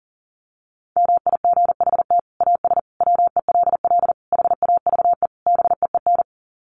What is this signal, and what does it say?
Morse code, heard as a keyed tone, about 40 wpm, 700 Hz. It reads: MIGHT AS WELL HAVE BEEN
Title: Sets of 5 Words Encore2 40wpm